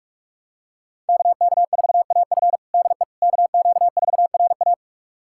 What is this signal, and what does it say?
Morse code, heard as a keyed tone, about 46 wpm, 700 Hz. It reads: KK4AF DE KQ4RA
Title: USA Callsign DE Callsign - 5 Characters - 45wpm